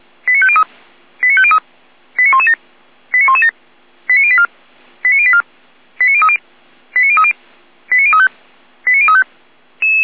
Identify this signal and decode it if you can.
Five-tone selective call (ZVEI), heard as quick runs of five tones, tones of about 70 ms. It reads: 89503 89503 89117 89117 89084 89084 89030 89030 89035 89035
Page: /455/SDR/ZVEI/